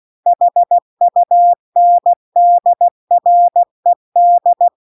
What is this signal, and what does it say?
Morse code, heard as a keyed tone, about 16 wpm, 700 Hz. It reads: HUNDRED
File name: CW-The-Big-Twoheaded-River-2-16WPM